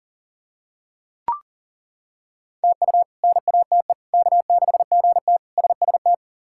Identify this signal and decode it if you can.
Morse code, heard as a keyed tone, about 40 wpm, 700 Hz. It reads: TU NATE K6GT SST
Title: K1USN SST QSO Practice - 40wpm